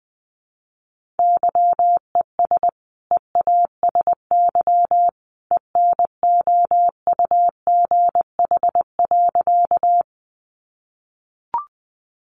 Morse code, heard as a keyed tone, about 20 wpm, 700 Hz. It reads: YES EASY ENOUGH.
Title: Sentences from Top 700 Words 20wpm